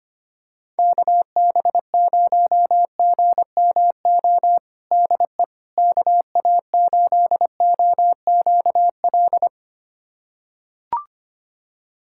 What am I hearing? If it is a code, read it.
Morse code, heard as a keyed tone, about 25 wpm, 700 Hz. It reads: KB0GMO DE KA8OQL
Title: USA Callsign DE Callsign 25wpm